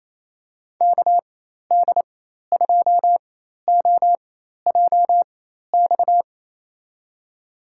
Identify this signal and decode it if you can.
Morse code, heard as a keyed tone, about 27 wpm, spaced as for 13 wpm, 700 Hz. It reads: KD2OJX
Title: USA Call Signs - 6 Characters - 4x Character Spacing - 28wpm